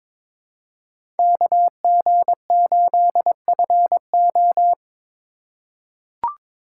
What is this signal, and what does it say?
Morse code, heard as a keyed tone, about 22 wpm, 700 Hz. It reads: KG8FO
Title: USA Call Signs - 4x Character Spacing - 22wpm